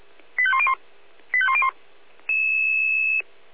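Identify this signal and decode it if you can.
Five-tone selective call (ZVEI), heard as quick runs of five tones, tones of about 70 ms. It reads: 85101 85101
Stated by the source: ZVEI five-tone selective call 85849, received with an SDR.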